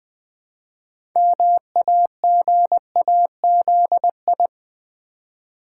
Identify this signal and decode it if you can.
Morse code, heard as a keyed tone, about 20 wpm, 700 Hz. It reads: MAGAZI
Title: Build A Word - 2 Letters At A Time - 8-Letter Words - 20wpm